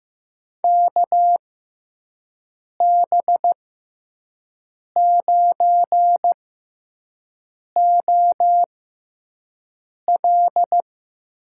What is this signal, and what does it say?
Morse code, heard as a keyed tone, about 15 wpm, spaced as for 5 wpm, 700 Hz. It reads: KB9OL
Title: USA Call Signs - 6 Characters - 6x Character Spacing - 15wpm